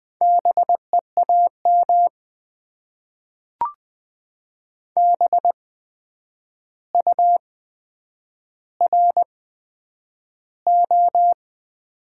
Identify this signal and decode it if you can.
Morse code, heard as a keyed tone, about 20 wpm, spaced as for 9 wpm, 700 Hz. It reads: BEAM BURO
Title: Top 90 QSO Elements - 8x Character Spacing - 20wpm